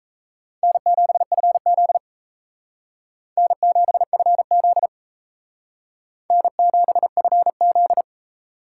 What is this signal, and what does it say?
Morse code, heard as a keyed tone, about 35 wpm, 700 Hz. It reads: N7FZ N7FZ N7FZ
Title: Speed Racing - USA 1x2 Call Signs 28wpm